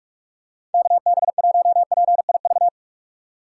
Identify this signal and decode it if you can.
Morse code, heard as a keyed tone, about 46 wpm, 690 Hz. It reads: KB1PIV
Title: USA Call Signs - 6 Characters - 3x Character Spacing - 45wpm